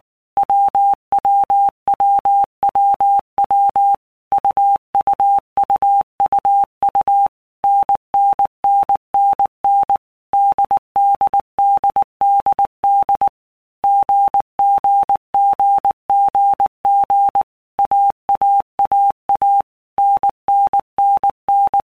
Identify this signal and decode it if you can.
Morse code, heard as a keyed tone, about 19 wpm, 800 Hz. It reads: WWWWW UUUUU NNNNN DDDDD GGGGG AAAA NNNN
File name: Basic Letter Groups By Blocks / Groups@18 / AWUNDG@18